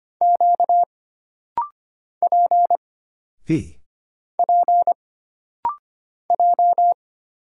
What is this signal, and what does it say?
Morse code, heard as a keyed tone, about 25 wpm, 700 Hz. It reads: Q P P J
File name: Single Character (Letter:Number:Punctuation:Prosign) - Rapid-Fire - 25wpm